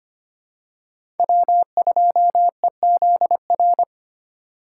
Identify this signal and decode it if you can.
Morse code, heard as a keyed tone, about 25 wpm, 700 Hz. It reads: W2EZR
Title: USA Call Signs - 5 Characters - 4x Character Spacing - 25wpm